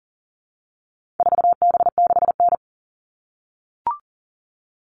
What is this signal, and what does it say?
Morse code, heard as a keyed tone, about 40 wpm, 700 Hz. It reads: 4B6N
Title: All Cut Numbers - 4-Digit 40wpm